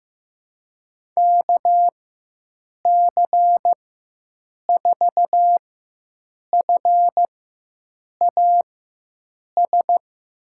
Morse code, heard as a keyed tone, about 15 wpm, spaced as for 7 wpm, 700 Hz. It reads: KC4FAS
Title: USA Call Signs - 4x Character Spacing - 15wpm